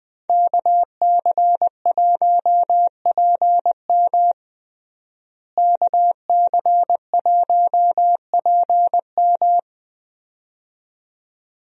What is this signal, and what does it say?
Morse code, heard as a keyed tone, about 20 wpm, 700 Hz. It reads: KC1PM KC1PM
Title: USA Call Signs - 5 Characters - Repeated 2x - 20wpm